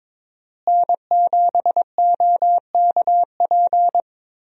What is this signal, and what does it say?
Morse code, heard as a keyed tone, about 22 wpm, 700 Hz. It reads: N7OKP
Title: USA Call Signs - 5 Characters - Repeated 2x - 22wpm